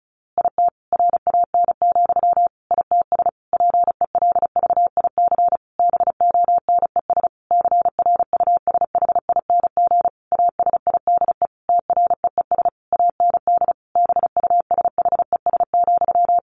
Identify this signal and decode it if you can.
Morse code, heard as a keyed tone, about 34 wpm, 700 Hz. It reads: IT RAN, ITS PELVIC BONES CRUSHING ASIDE TREES AND BUSHES,
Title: A Sound Of Thunder 35wpm